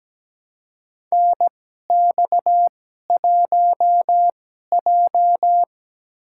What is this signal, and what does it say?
Morse code, heard as a keyed tone, about 17 wpm, spaced as for 12 wpm, 700 Hz. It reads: NX1J
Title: USA Call Signs - 2x Character Spacing - 17wpm